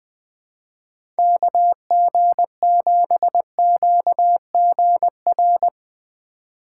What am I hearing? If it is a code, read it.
Morse code, heard as a keyed tone, about 20 wpm, 700 Hz. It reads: KG7QGR
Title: USA Call Signs - 4x Character Spacing - 20wpm